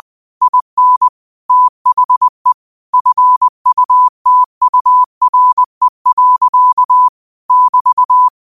Morse code, heard as a keyed tone, about 20 wpm, 1000 Hz. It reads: IN THE FUTURE. <BT>